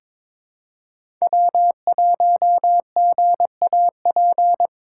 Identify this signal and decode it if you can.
Morse code, heard as a keyed tone, about 22 wpm, 700 Hz. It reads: W1GAP
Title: USA Call Signs - 5 Characters - 3x Character Spacing - 22wpm